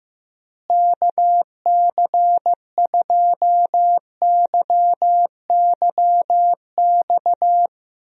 Morse code, heard as a keyed tone, about 15 wpm, 700 Hz. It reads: KC2YYX
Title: USA Call Signs - 6 Characters - Repeated 2x - 15wpm